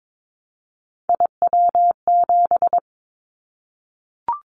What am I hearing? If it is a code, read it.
Morse code, heard as a keyed tone, about 22 wpm, 700 Hz. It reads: IW7
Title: Callsign Prefixes for Top 25 DXCC - 22wpm